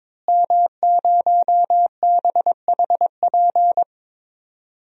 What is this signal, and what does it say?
Morse code, heard as a keyed tone, about 22 wpm, 700 Hz. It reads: M0BHP
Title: DXCC-100 Call Signs - 5 Characters - Repeated 3x - 22wpm